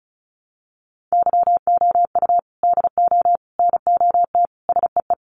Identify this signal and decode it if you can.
Morse code, heard as a keyed tone, about 34 wpm, 700 Hz. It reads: YOU DO NOT SEE
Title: Sentences from Top 1300 Words 35wpm